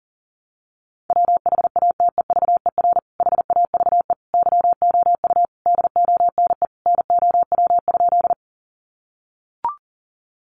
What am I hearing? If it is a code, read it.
Morse code, heard as a keyed tone, about 40 wpm, 700 Hz. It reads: WHATEVER HAVE YOU DONE NOW?
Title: Sentences from Top 1100 Words 40wpm